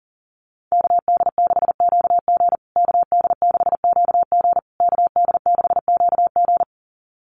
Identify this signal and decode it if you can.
Morse code, heard as a keyed tone, about 40 wpm, 700 Hz. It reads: KD6QG KD6QG KD6QG
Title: USA Call Signs Repeated 3x 40wpm